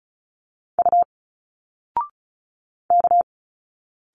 Morse code, heard as a keyed tone, about 36 wpm, 700 Hz. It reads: U K
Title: Single Letters 35wpm